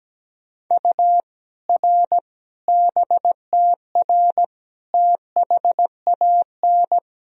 Morse code, heard as a keyed tone, about 17 wpm, 700 Hz. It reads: U R BTR THAN
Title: Sentences from Top 400 Words and Abbreviations 17wpm